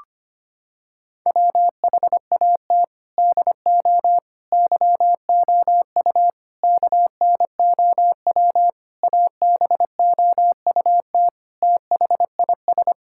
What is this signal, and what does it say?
Morse code, heard as a keyed tone, about 25 wpm, 700 Hz. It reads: WHAT DO YOU KNOW ABOUT THIS
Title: Sentences from Top 1000 Words - Review Part 6 - 25wpm